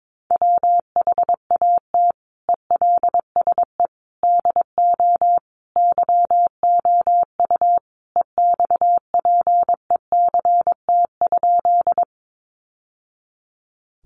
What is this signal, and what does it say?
Morse code, heard as a keyed tone, about 22 wpm, 700 Hz. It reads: WHAT ELSE DO YOU EXPECT?
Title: Sentences from Top 1100 Words 22wpm